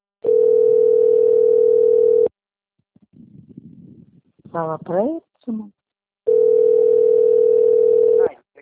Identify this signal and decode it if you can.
Ringback tone (North America), 440 + 480 Hz, with speech.
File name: rg-1600-unknown- - - .44059